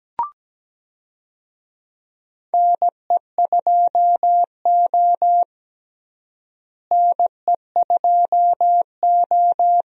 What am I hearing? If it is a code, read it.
Morse code, heard as a keyed tone, about 17 wpm, 700 Hz. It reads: NE2O NE2O
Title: USA Call Signs - 4 Characters - Repeated 3x - 17wpm